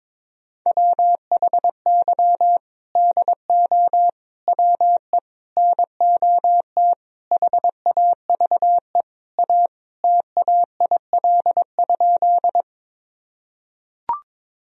Morse code, heard as a keyed tone, about 22 wpm, 700 Hz. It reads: WHY DO WE NOT HAVE A TAIL?
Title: Sentences from Top 1000 Words - Review Part 3 - 22wpm